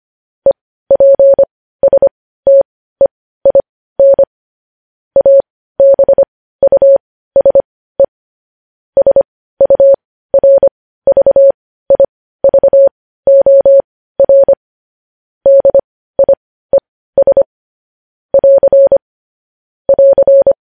Morse code, heard as a keyed tone, about 25 wpm, spaced as for 15 wpm, 550 Hz. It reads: EPSTEIN ABUSE SURVIVOR DIES <AR> <AR>
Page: News headlines at 15